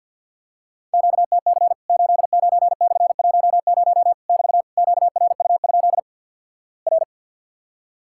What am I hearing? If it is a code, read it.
Morse code, heard as a keyed tone, about 50 wpm, 700 Hz. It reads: QTC 89/10 <BT> QRU? R
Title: Worked All Europe DX Contest - QTC Exchange with Repeats - 50wpm